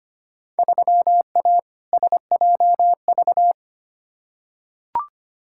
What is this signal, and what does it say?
Morse code, heard as a keyed tone, about 25 wpm, 700 Hz. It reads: 3A SJV
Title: Field Day Exchanges Encore - 25wpm